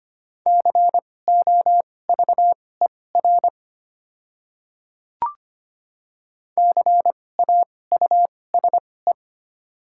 Morse code, heard as a keyed tone, about 25 wpm, spaced as for 18 wpm, 690 Hz. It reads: COVER CAUSE
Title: 2x Character Spacing - Top 300 Words - 25wpm